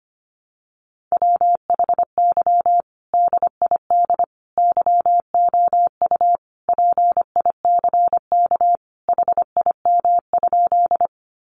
Morse code, heard as a keyed tone, about 25 wpm, 700 Hz. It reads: WHY DID YOU PICK HIM?
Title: Sentences from Top 1000 Words 25wpm